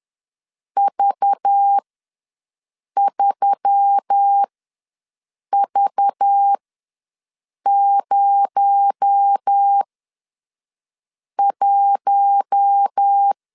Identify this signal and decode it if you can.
Morse code, heard as a keyed tone, about 10 wpm, spaced as for 5 wpm, 800 Hz. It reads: V3V01
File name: LEZIONI / 6 PKX0 / 6 PKX0 ws 10wpm 4220